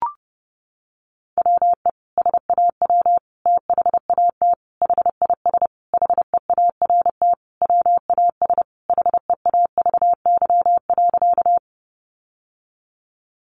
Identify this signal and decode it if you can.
Morse code, heard as a keyed tone, about 30 wpm, 700 Hz. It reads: WE SAW THAT HIS HEART WAS HEAVY.
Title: Sentences from Top 1200 Words 30wpm